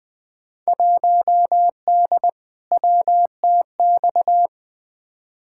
Morse code, heard as a keyed tone, about 20 wpm, 700 Hz. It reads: 1D WTX
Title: Field Day Exchanges - 20wpm